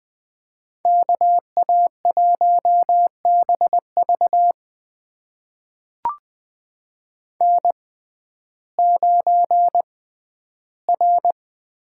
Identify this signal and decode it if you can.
Morse code, heard as a keyed tone, about 20 wpm, spaced as for 12 wpm, 700 Hz. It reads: KA1BV N9R
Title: USA Call Signs - 5 Characters - 6x Character Spacing - 20wpm